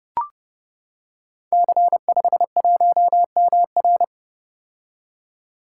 Morse code, heard as a keyed tone, about 30 wpm, 700 Hz. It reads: C51MR
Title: Code Groups - 5 Letter:Numbers - 30wpm